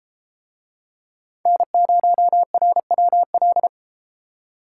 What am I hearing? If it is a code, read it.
Morse code, heard as a keyed tone, about 30 wpm, 700 Hz. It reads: N0RWL